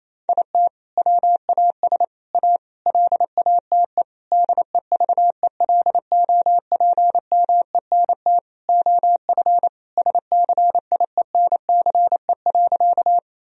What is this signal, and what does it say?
Morse code, heard as a keyed tone, about 28 wpm, 700 Hz. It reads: IT WAS A LATE DEVELOPMENT OF SCIENCE.
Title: Sentences from Top 1000 Words - Review Part 4 - 28wpm